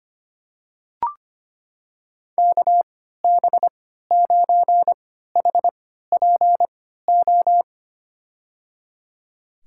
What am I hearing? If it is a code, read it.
Morse code, heard as a keyed tone, about 25 wpm, spaced as for 14 wpm, 690 Hz. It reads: KB9HPO
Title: USA Call Signs - 6 Characters - 3x Character Spacing - 25wpm